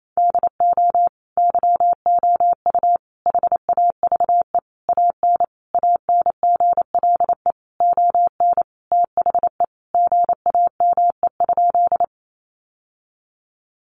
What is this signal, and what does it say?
Morse code, heard as a keyed tone, about 28 wpm, 700 Hz. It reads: DO YOU HAVE AN ANGLE ON THE GAME?
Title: Sentences from Top 1500 Words 28wpm